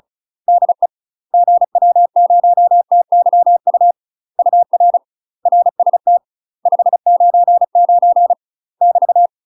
Morse code, heard as a keyed tone, about 34 wpm, 700 Hz. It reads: DE GW0TYU UR RST 599 <BT>
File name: UK MASTER QSOs TRAINING / 35WPM / UK MASTER QSOs 0039